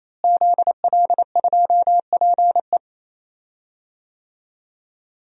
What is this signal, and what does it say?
Morse code, heard as a keyed tone, about 28 wpm, 700 Hz. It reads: ZL2PE
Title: DXCC-100 Call Signs - 5 Characters - Repeated 3x - 28wpm